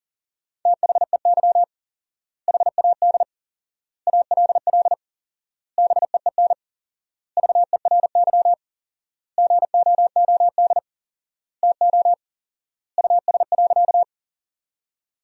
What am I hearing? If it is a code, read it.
Morse code, heard as a keyed tone, about 37 wpm, 700 Hz. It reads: THEY HAD ALL BEEN VERY GOOD TO US.
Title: 4x Word-Spacing - Sentences from Top 200 Words - 40wpm